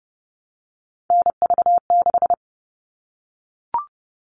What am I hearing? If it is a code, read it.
Morse code, heard as a keyed tone, about 30 wpm, 700 Hz. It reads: NV6
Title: All Cut Numbers - 3-Digit 30wpm